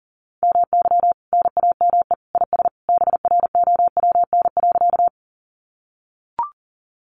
Morse code, heard as a keyed tone, about 40 wpm, 700 Hz. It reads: MY NAME IS BROWN.
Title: Sentences from Top 1200 Words 40wpm